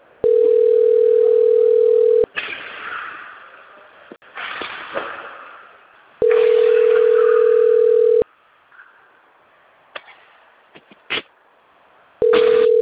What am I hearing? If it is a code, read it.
Ringback tone (North America), 440 + 480 Hz.